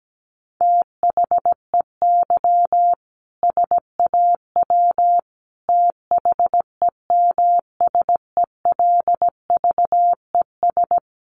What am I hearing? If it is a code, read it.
Morse code, heard as a keyed tone, about 17 wpm, 700 Hz. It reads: THEY SAW THEMSELVES
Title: A Sound Of Thunder 17wpm